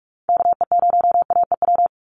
Morse code, heard as a keyed tone, about 44 wpm, 700 Hz. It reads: KE0AEW
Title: USA Call Signs 45wpm